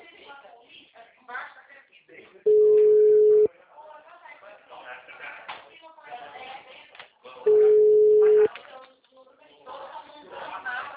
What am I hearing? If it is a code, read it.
Ringback tone (Europe, 425 Hz), with speech.
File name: out- -2021- - - .80131